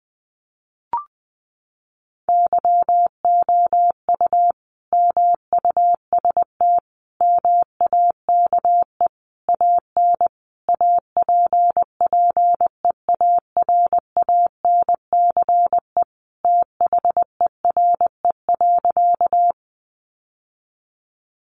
Morse code, heard as a keyed tone, about 20 wpm, 700 Hz. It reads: YOU MUST MAKE AN APPEARANCE THERE.
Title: Sentences from Top 1100 Words 20wpm